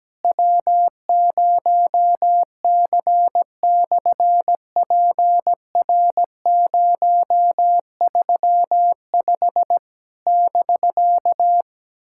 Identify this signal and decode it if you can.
Morse code, heard as a keyed tone, about 17 wpm, 700 Hz. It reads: W0C/PR035 <BK>
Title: SOTA QSO Practice 17wpm